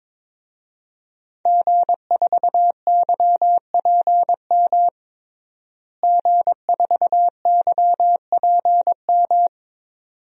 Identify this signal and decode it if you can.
Morse code, heard as a keyed tone, about 22 wpm, 700 Hz. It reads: G4YPM G4YPM
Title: DXCC-100 Call Signs - 5 Characters - Repeated 2x - 22wpm